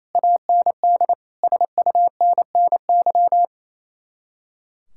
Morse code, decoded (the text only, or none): AND SUNNY